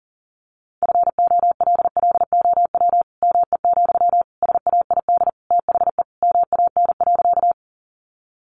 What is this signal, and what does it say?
Morse code, heard as a keyed tone, about 40 wpm, 700 Hz. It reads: FOLLOW ME, SAID THE MAN.
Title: Sentences from Top 200 Words 40wpm